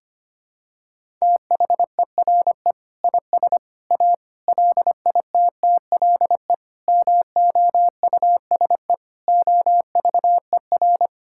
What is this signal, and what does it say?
Morse code, heard as a keyed tone, about 25 wpm, 700 Hz. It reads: THERE IS A LITTLE MOUSE OVER
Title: Sentences from Top 1000 Words - Review Part 5 - 25wpm